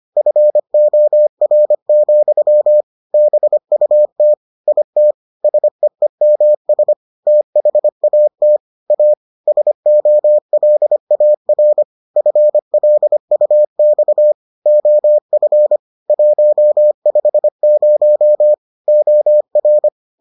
Morse code, heard as a keyed tone, about 25 wpm, 600 Hz. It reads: FOR, BUT IT SEEMS THAT A SOLAR FLUX OF 150 OR